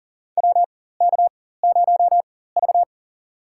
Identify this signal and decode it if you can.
Morse code, heard as a keyed tone, about 40 wpm, spaced as for 19 wpm, 700 Hz. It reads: WK0V
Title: USA Call Signs - 4 Characters - 4x Character Spacing - 40wpm